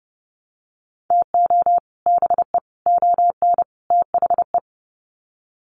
Morse code, heard as a keyed tone, about 29 wpm, 700 Hz. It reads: TO BE ON THE
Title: Sets of 4 Words 30wpm